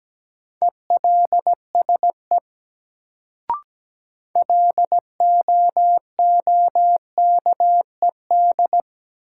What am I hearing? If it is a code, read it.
Morse code, heard as a keyed tone, about 17 wpm, 700 Hz. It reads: ELSE LOOKED
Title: Sentences from Top 800 Words - New Word Drill - 17wpm